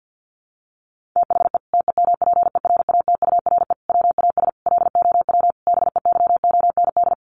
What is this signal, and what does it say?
Morse code, heard as a keyed tone, about 50 wpm, 700 Hz. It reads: THE TEMPERATURE WAS LOW BEYOND
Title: Sentences from Top 1500 Words 50wpm